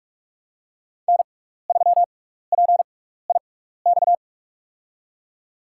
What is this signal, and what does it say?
Morse code, heard as a keyed tone, about 47 wpm, spaced as for 16 wpm, 690 Hz. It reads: N3PIX
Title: USA Call Signs - 5 Characters - 6x Character Spacing - 45wpm